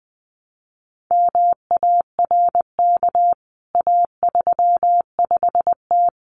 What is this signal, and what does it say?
Morse code, heard as a keyed tone, about 20 wpm, 700 Hz. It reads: MARK A35T
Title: CWT Exchange 20wpm